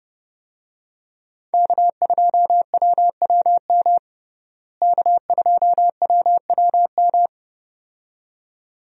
Morse code, heard as a keyed tone, about 31 wpm, 700 Hz. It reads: K2WWM K2WWM
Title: USA Call Signs - 5 Characters - Repeated 2x - 30wpm